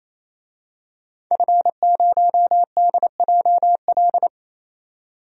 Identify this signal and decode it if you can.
Morse code, heard as a keyed tone, about 28 wpm, 700 Hz. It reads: F0DJL
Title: DXCC-100 Call Signs - 5 Characters - 28wpm